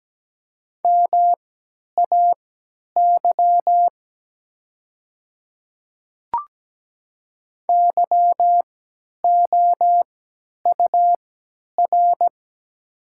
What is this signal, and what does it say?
Morse code, heard as a keyed tone, about 17 wpm, spaced as for 10 wpm, 690 Hz. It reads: MAY YOUR
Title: 3x Character Spacing - Top 500 Words - 17wpm